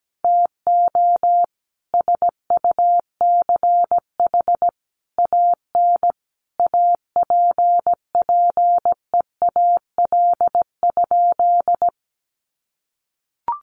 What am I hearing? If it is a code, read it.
Morse code, heard as a keyed tone, about 17 wpm, 700 Hz. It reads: TO SUCH AN APPEAL?
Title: Sentences from Top 1300 Words 17wpm